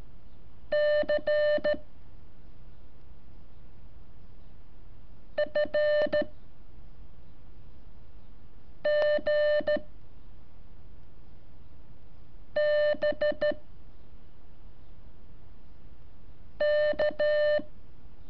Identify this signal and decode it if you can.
Morse code, heard as a keyed tone, about 13 wpm, 600 Hz. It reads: C F G B K
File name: curso / ejercicio 21